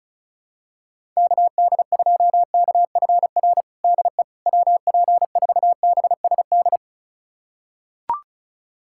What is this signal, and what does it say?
Morse code, heard as a keyed tone, about 36 wpm, 700 Hz. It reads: KD2KFR DE WP4BSD
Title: USA Callsign DE Callsign 35wpm